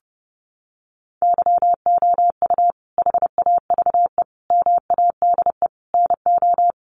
Morse code, heard as a keyed tone, about 30 wpm, 700 Hz. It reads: YOU HAVE MADE NO
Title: Sentences from Top 1400 Words 30wpm